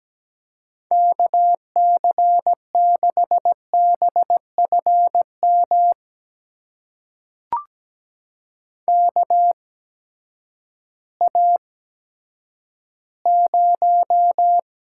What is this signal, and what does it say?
Morse code, heard as a keyed tone, about 17 wpm, spaced as for 10 wpm, 700 Hz. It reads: KC6BFM KA0
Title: USA Call Signs - 6 Characters - 8x Character Spacing - 17wpm